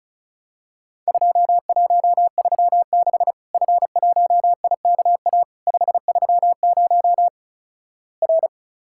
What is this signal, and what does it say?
Morse code, heard as a keyed tone, about 36 wpm, 700 Hz. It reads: 2136 F1IKA 530 R
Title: Worked All Europe DX Contest - QTC Exchange - 35wpm